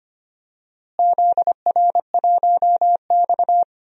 Morse code, heard as a keyed tone, about 25 wpm, 700 Hz. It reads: ZR1X
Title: DXCC-100 Call Signs - 4 Characters - Repeated 3x - 25wpm